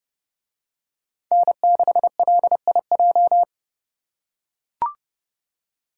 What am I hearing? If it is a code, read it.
Morse code, heard as a keyed tone, about 29 wpm, 700 Hz. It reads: N6LIJ